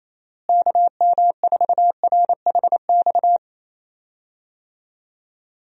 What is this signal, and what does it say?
Morse code, heard as a keyed tone, about 27 wpm, 700 Hz. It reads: KM4RHX
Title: USA Call Signs - 6 Characters - Repeated 3x - 28wpm